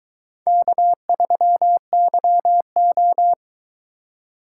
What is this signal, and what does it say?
Morse code, heard as a keyed tone, about 22 wpm, 700 Hz. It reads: K3YO